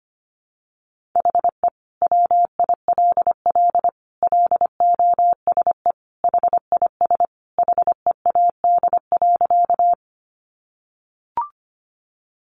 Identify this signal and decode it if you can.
Morse code, heard as a keyed tone, about 25 wpm, 700 Hz. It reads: HE WILL LOSE HIS HEAD.
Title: Sentences from Top 1100 Words 25wpm